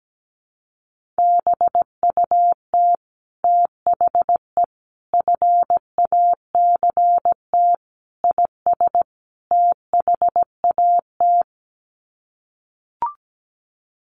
Morse code, heard as a keyed tone, about 17 wpm, 700 Hz. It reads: BUT THE FACT IS THAT
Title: Sets of 5 Words Encore2 17wpm